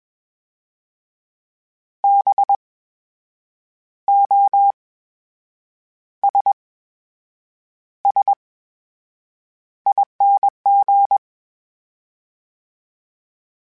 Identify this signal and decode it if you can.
Morse code, heard as a keyed tone, about 21 wpm, spaced as for 7 wpm, 800 Hz. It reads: BOSSING
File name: Special INGs-mp3s / Special INGs-mp3s / VerySlowToFaster Ing @20